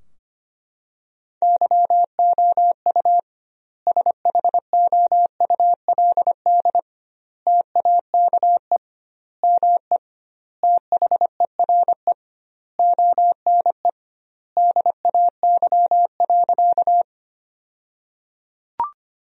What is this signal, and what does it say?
Morse code, heard as a keyed tone, about 25 wpm, 700 Hz. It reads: YOU SHOULD TAKE ME THERE ONE DAY.
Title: 2x Word-Spacing - Sentences from Top 300 Words - 25wpm